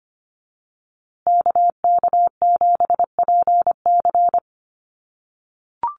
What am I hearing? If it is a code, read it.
Morse code, heard as a keyed tone, about 25 wpm, 700 Hz. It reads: KK7PC
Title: USA Call Signs Repeated 3x 25wpm